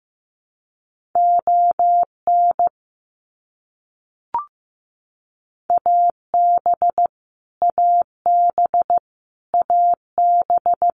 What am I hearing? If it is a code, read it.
Morse code, heard as a keyed tone, about 15 wpm, 700 Hz. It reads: ON AB AB AB
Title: Canadian Provinces and Territories - Repeated 3x - 15wpm